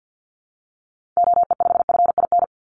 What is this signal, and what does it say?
Morse code, heard as a keyed tone, about 50 wpm, 700 Hz. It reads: OE5FIN
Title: Call Signs - 50wpm